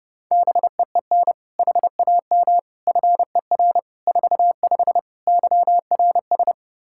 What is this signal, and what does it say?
Morse code, heard as a keyed tone, about 30 wpm, 700 Hz. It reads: BEEN HAM FER 45 YRS
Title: Practice QSO - Exchange 3 - More Info - 30wpm